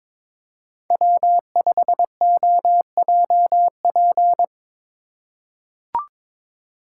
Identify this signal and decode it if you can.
Morse code, heard as a keyed tone, about 21 wpm, 700 Hz. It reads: W5OJP
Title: USA Call Signs - 8x Character Spacing - 22wpm